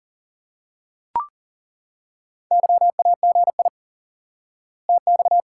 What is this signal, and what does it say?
Morse code, heard as a keyed tone, about 40 wpm, 700 Hz. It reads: YAGI TX
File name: IQE Course 3 - Lesson 89 - YAGI - Two Elements - 40wpm